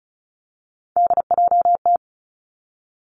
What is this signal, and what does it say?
Morse code, heard as a keyed tone, about 36 wpm, 700 Hz. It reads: DJT
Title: Three Challenging Letter-Number Combinations 35wpm